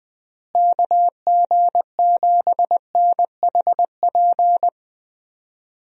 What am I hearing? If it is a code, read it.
Morse code, heard as a keyed tone, about 20 wpm, 700 Hz. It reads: KG7NHP
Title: USA Call Signs - 6 Characters - Repeated 3x - 20wpm